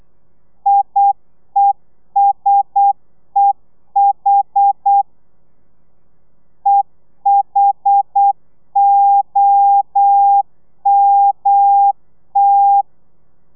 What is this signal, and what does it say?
Morse code, heard as a keyed tone, about 8 wpm, 800 Hz. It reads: IESEH EHOMT
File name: telegrafia / i2tfj / Lez1 35